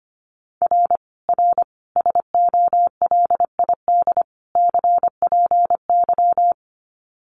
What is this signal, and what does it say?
Morse code, heard as a keyed tone, about 25 wpm, 700 Hz. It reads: R R SOLID CPY
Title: Common QSO Phrases 25wpm